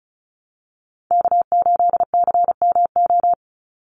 Morse code, heard as a keyed tone, about 34 wpm, 700 Hz. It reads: K8CMO
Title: Call Signs - 35wpm